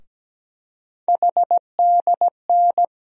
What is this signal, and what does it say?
Morse code, heard as a keyed tone, about 17 wpm, 700 Hz. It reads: HDN
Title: USA Call Sign Postfixes - 3 Letters 17wpm